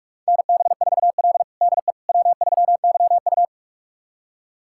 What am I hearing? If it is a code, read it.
Morse code, heard as a keyed tone, about 46 wpm, 700 Hz. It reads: NB4L DE W3YU
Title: USA Callsign DE Callsign - 4 Characters - 45wpm